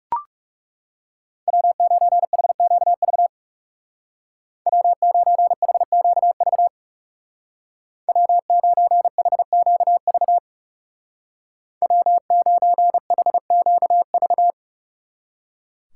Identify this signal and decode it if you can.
Morse code, heard as a keyed tone, about 35 wpm, 700 Hz. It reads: W9HQV W9HQV W9HQV W9HQV
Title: Speed Racing - USA Call Signs 30wpm